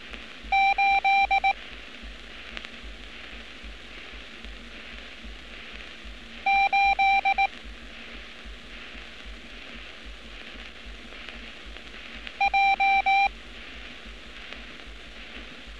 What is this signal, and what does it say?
Morse code, heard as a keyed tone, about 18 wpm, 780 Hz. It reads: 8 8 J
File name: training / NAVPERS 11200-11211 / NAVPERS 11206 RA RB